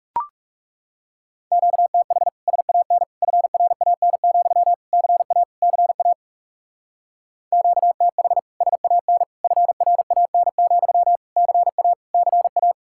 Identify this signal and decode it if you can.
Morse code, heard as a keyed tone, about 39 wpm, 700 Hz. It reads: QTH SAN FRAN, CA CA QTH SAN FRAN, CA CA